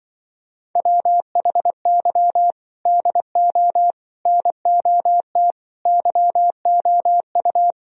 Morse code, heard as a keyed tone, about 24 wpm, 700 Hz. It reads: WHY DO NOT YOU
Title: Speed-Racing - Sets of 4 Words 20wpm